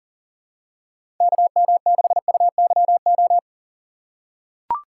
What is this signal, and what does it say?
Morse code, heard as a keyed tone, about 40 wpm, 700 Hz. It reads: KM6UYO
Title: USA Call Signs - 2x Character Spacing - 40wpm